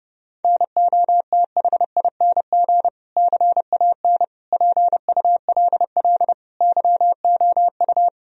Morse code, heard as a keyed tone, about 30 wpm, 700 Hz. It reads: NOTHING CAN PULL YOU
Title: Sentences from Top 1000 Words - Review Part 2 - 30wpm